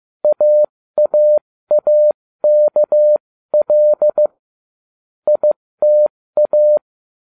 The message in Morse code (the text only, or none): AAAKL ITA